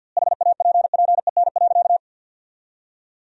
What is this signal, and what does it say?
Morse code, heard as a keyed tone, about 50 wpm, 700 Hz. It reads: HAPPEN.
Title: 2x Word-Spacing - Sentences from Top 400 Words - 50wpm